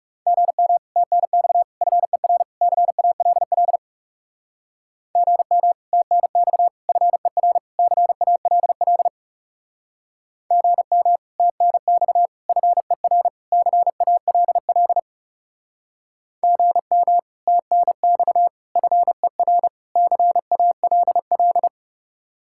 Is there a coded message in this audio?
Yes: Morse code from 0.3 to 21.7 s, heard as a keyed tone, about 38 wpm, 700 Hz. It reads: GM TNX FER CALL GM TNX FER CALL GM TNX FER CALL GM TNX FER CALL